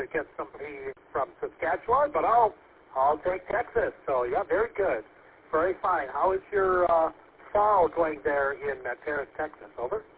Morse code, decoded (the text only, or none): none